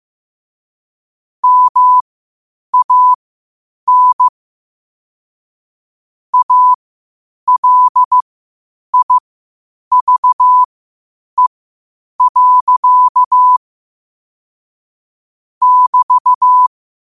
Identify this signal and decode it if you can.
Morse code, heard as a keyed tone, about 15 wpm, spaced as for 8 wpm, 1000 Hz. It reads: MAN ALIVE. <BT>